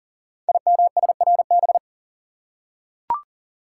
Morse code, heard as a keyed tone, about 38 wpm, 690 Hz. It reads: IMSRB
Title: Code Groups - 5 Letters - 40wpm